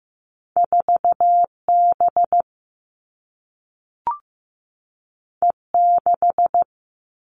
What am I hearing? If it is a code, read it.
Morse code, heard as a keyed tone, about 15 wpm, 700 Hz. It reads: 4B E6
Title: All Cut Numbers - 2-Digit 15wpm